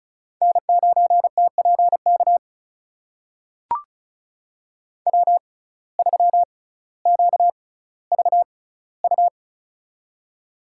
Morse code, heard as a keyed tone, about 34 wpm, spaced as for 18 wpm, 700 Hz. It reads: N9TPK W3QVU
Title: USA Call Signs - 5 Characters - 6x Character Spacing - 35wpm